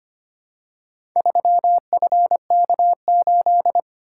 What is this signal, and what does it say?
Morse code, heard as a keyed tone, about 25 wpm, 700 Hz. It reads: 3FK8
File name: practice / 25wpm / Four Letter-Number Combinations - 25wpm